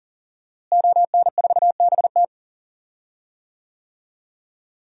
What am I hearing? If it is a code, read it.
Morse code, heard as a keyed tone, about 40 wpm, 700 Hz. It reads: ON4BT
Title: DXCC-100 Call Signs - 5 Characters - Repeated 2x - 40wpm